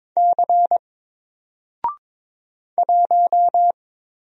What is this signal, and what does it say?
Morse code, heard as a keyed tone, about 22 wpm, 700 Hz. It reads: C 1
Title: Single Character (Letter:Number:Punctuation:Prosign) 22wpm